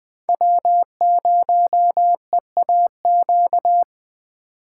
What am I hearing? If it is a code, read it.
Morse code, heard as a keyed tone, about 21 wpm, 700 Hz. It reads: W0EAQ